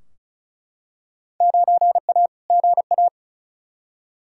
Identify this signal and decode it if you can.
Morse code, heard as a keyed tone, about 34 wpm, 700 Hz. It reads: 9A GA
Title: Field Day Exchanges Encore - 35wpm